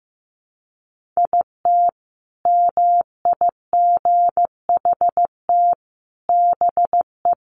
Morse code, heard as a keyed tone, about 15 wpm, 700 Hz. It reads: IT MIGHT BE
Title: Sentences from Top 1500 Words 15wpm